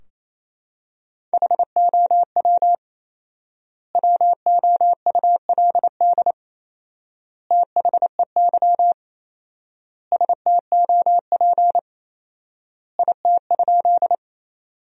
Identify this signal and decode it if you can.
Morse code, heard as a keyed tone, about 28 wpm, 700 Hz. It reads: HOW WOULD THEY STOP IT?
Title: 4x Word-Spacing - Sentences from Top 300 Words - 28wpm